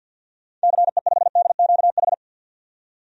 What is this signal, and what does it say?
Morse code, heard as a keyed tone, about 51 wpm, 700 Hz. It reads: CE5DQH
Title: DXCC-100 Call Signs - 6 Characters - 50wpm